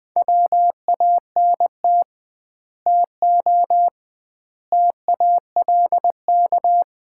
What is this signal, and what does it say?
Morse code, heard as a keyed tone, about 20 wpm, 700 Hz. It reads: WANT TO TALK
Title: 2x Word-Spacing - Sets of 4 Words - 20wpm